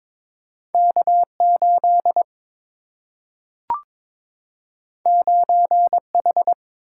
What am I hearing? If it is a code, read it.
Morse code, heard as a keyed tone, about 22 wpm, 700 Hz. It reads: K8 9H
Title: Single-Letter Single-Number 22wpm